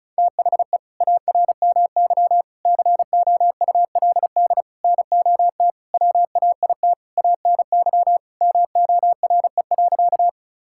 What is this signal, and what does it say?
Morse code, heard as a keyed tone, about 34 wpm, 700 Hz. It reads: THE ARMY COULD NOT WAIT ANY MORE.
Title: Sentences from Top 1000 Words - Review Part 5 - 35wpm